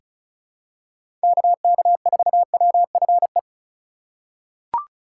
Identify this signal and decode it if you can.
Morse code, heard as a keyed tone, about 34 wpm, 700 Hz. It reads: KK4WFE
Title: USA Call Signs - 6 Characters - 4x Character Spacing - 35wpm